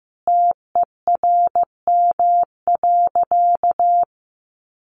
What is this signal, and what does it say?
Morse code, heard as a keyed tone, about 15 wpm, 700 Hz. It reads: TERM.
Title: Sentences from Top 1100 Words 15wpm